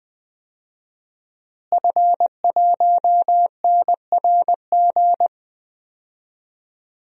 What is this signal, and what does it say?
Morse code, heard as a keyed tone, about 20 wpm, 700 Hz. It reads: F1NRG
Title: DXCC-100 Call Signs - 5 Characters - Repeated 3x - 20wpm